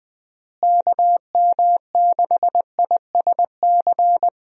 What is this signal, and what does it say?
Morse code, heard as a keyed tone, about 20 wpm, 700 Hz. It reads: KM6ISC
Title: USA Call Signs - 6 Characters - Repeated 3x - 20wpm